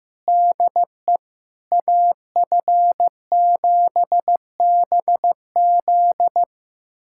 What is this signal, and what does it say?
Morse code, heard as a keyed tone, about 15 wpm, 700 Hz. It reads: DE AF7BZ
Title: USA Callsign DE Callsign - 5 Characters - 15wpm